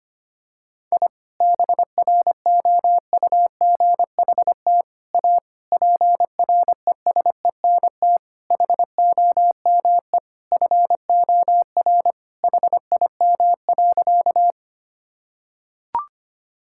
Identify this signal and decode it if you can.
Morse code, heard as a keyed tone, about 25 wpm, 700 Hz. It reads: I BROUGHT A PRESENT HOME FOR HIM.
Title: Sentences from Top 1000 Words - Review Part 5 - 25wpm